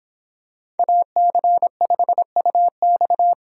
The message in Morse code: AC5UX

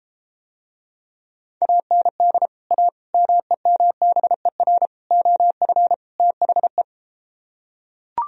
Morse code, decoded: AND A MEMBER OF THE

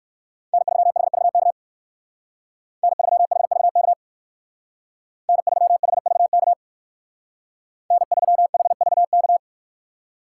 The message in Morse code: N3HVK N3HVK N3HVK N3HVK

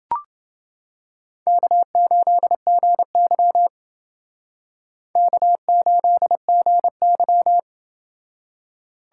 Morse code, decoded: K8GY K8GY

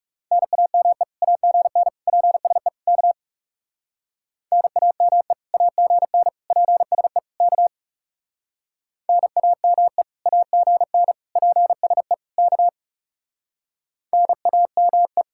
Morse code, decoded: NAME AGN PSE K NAME AGN PSE K NAME AGN PSE K NAME

